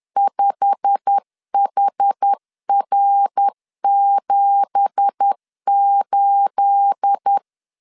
Morse code, heard as a keyed tone, about 10 wpm, 800 Hz. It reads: 5HR78
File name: LEZIONI / 9 2867 10wpm 2733